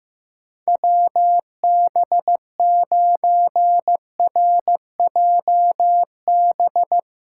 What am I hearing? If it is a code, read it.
Morse code, heard as a keyed tone, about 15 wpm, 700 Hz. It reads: WB9RJB
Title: USA Callsign DE Callsign 15wpm